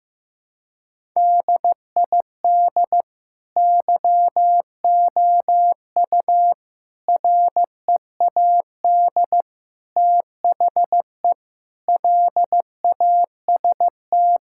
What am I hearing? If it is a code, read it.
Morse code, heard as a keyed tone, about 15 wpm, 700 Hz. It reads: DID YOU READ THE LAST
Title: Sentences from Top 1000 Words - Review Part 5 - 15wpm